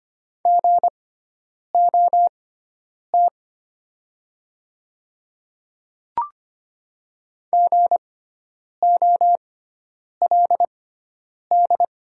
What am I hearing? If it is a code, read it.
Morse code, heard as a keyed tone, about 25 wpm, spaced as for 9 wpm, 700 Hz. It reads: GOT GOLD
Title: 6x Character Spacing - Top 500 Words - 25wpm